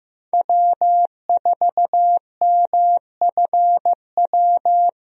Morse code, heard as a keyed tone, about 15 wpm, 700 Hz. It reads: W4MFW
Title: USA Call Signs - 5 Characters - Repeated 2x - 15wpm